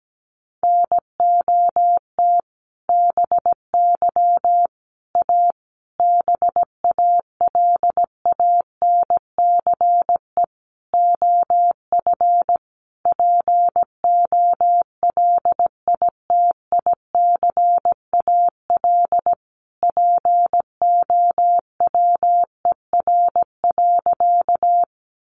Morse code, heard as a keyed tone, about 17 wpm, 700 Hz. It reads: NOT BY A BALANCE OF POLITICAL POWER.
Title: Sentences from Top 900 Words 17wpm